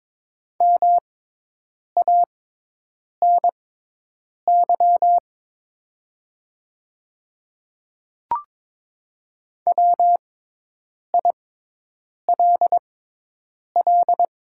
Morse code, heard as a keyed tone, about 22 wpm, spaced as for 8 wpm, 690 Hz. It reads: MANY WILL
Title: 6x Character Spacing - Top 100 Words - 22wpm